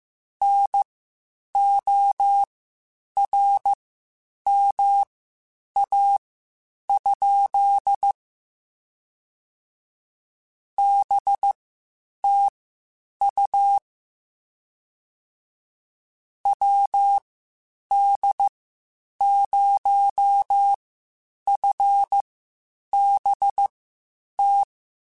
Morse code, heard as a keyed tone, about 15 wpm, spaced as for 8 wpm, 800 Hz. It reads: NORMA? BTU WD0FBT
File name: (spaced) / QSOs Spaced-mp3s / WD0FBT 1 @13 sp